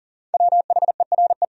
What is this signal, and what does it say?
Morse code, heard as a keyed tone, about 38 wpm, 700 Hz. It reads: WHERE
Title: Sentences from Top 1000 Words - Review Part 1 - 40wpm